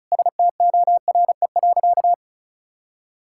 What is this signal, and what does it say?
Morse code, heard as a keyed tone, about 36 wpm, 700 Hz. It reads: STORE.
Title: Sentences from Top 1000 Words - Review Part 5 - 35wpm